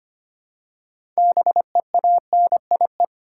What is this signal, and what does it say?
Morse code, heard as a keyed tone, about 25 wpm, 700 Hz. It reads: BEANIE
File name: practice / 25wpm / Difficult Words with Hints - 25wpm